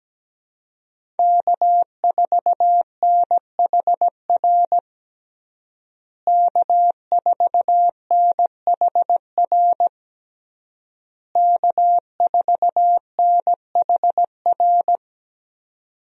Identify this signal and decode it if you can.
Morse code, heard as a keyed tone, about 17 wpm, 700 Hz. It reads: K4NHR K4NHR K4NHR
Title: USA Call Signs - 5 Characters - Repeated 3x - 17wpm